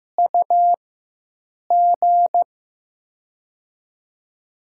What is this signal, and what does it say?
Morse code, heard as a keyed tone, about 15 wpm, spaced as for 7 wpm, 700 Hz. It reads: UG
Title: USA Call Signs - 4x Character Spacing - 15wpm